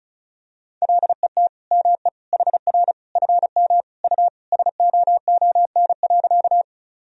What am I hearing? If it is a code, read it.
Morse code, heard as a keyed tone, about 34 wpm, 700 Hz. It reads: LET ME HR FM U SOON.
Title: Sentences from Top 400 Words and Abbreviations 35wpm